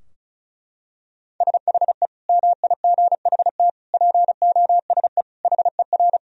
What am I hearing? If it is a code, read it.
Morse code, heard as a keyed tone, about 36 wpm, 700 Hz. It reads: SHE MIGHT POSE HER
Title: Sentences from Top 1000 Words - Review Part 2 - 35wpm